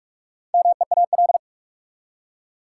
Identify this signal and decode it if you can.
Morse code, heard as a keyed tone, about 42 wpm, 700 Hz. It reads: MEAL
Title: Four Letter Words - 45wpm